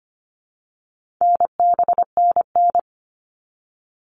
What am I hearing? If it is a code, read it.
Morse code, heard as a keyed tone, about 25 wpm, 700 Hz. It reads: NBNN
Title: All Cut Numbers - 4-Digit 25wpm